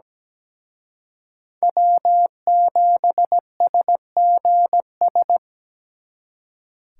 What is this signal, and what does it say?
Morse code, heard as a keyed tone, about 20 wpm, 700 Hz. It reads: W7SGS
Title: Speed Racing - USA Call Signs 17wpm